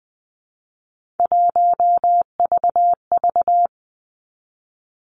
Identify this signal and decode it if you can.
Morse code, heard as a keyed tone, about 20 wpm, 700 Hz. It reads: 1VV
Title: Three Challenging Letter-Number Combinations 20wpm